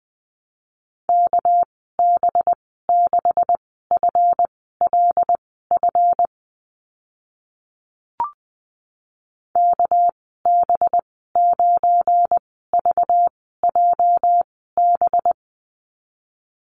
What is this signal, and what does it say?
Morse code, heard as a keyed tone, about 20 wpm, spaced as for 14 wpm, 700 Hz. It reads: KB6FLF KB9VJB
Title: 219 Lesson 35g - Number 0 - Call Signs 20wpm